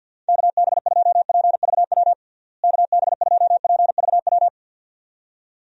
Morse code, heard as a keyed tone, about 49 wpm, 700 Hz. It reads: KB2PVW KB2PVW
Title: USA Call Signs - 6 Characters - Repeated 2x - 50wpm